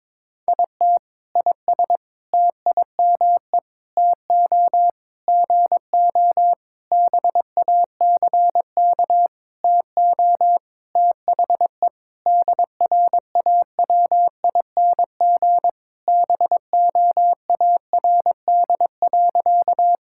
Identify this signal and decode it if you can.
Morse code, heard as a keyed tone, about 22 wpm, 700 Hz. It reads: IT IS TIME TO GO BACK TO THE DRAWING BOARD.
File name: Sentences with 7-Word Phrases - 22wpm